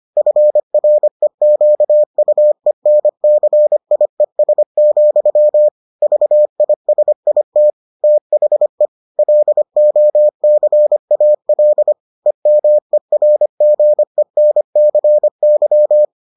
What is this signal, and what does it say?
Morse code, heard as a keyed tone, about 25 wpm, 600 Hz. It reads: FREQUENCIES, VISIT THE LOCAL EMERGENCY